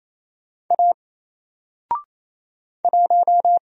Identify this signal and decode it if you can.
Morse code, heard as a keyed tone, about 28 wpm, 700 Hz. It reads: A 1
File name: Single-Letter-Number-28wpm